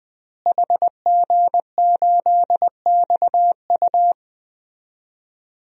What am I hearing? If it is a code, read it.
Morse code, heard as a keyed tone, about 20 wpm, 700 Hz. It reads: HG8XU
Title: DXCC-100 Call Signs - 5 Characters - Repeated 3x - 20wpm